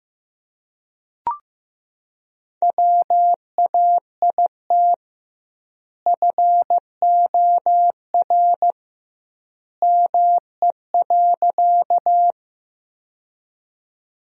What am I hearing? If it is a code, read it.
Morse code, heard as a keyed tone, about 15 wpm, 700 Hz. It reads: WAIT FOR ME.
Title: 2x Word-Spacing - Sentences from Top 500 Words - 15wpm